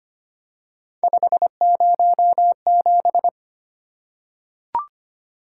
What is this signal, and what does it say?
Morse code, heard as a keyed tone, about 25 wpm, 700 Hz. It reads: 507
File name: Three Letter-Number Combinations - 25wpm